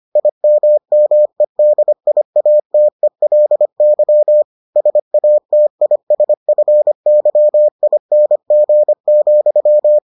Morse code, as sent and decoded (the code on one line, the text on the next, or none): .. -- -- . -.. .. .- - . .-.. -.-- / ... .- - .. ... ..-. -.-- .. -. --. --..--
IMMEDIATELY SATISFYING,